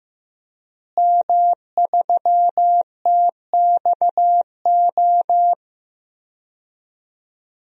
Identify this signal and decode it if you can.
Morse code, heard as a keyed tone, about 15 wpm, 700 Hz. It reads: M3TXO
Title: DXCC-100 Call Signs - 5 Characters - Repeated 2x - 15wpm